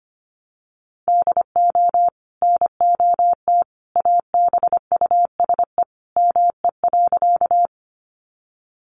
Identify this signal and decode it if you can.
Morse code, heard as a keyed tone, about 25 wpm, 700 Hz. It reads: DO NOT ABUSE ME.
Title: Sentences from Top 1300 Words 25wpm